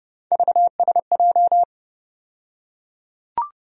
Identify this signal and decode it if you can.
Morse code, heard as a keyed tone, about 31 wpm, 700 Hz. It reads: VSJ
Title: USA Call Sign Postfixes - 3 Letters 30wpm